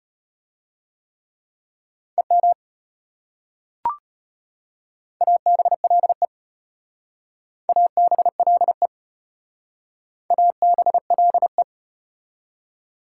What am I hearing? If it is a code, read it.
Morse code, heard as a keyed tone, about 33 wpm, 690 Hz. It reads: EM ABLE ABLE ABLE